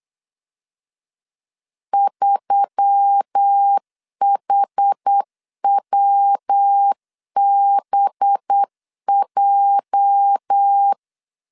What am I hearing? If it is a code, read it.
Morse code, heard as a keyed tone, about 8 wpm, 800 Hz. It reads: 3HWBJ
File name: LEZIONI / 8 YJ45 / 8 YJ 8wpm 3313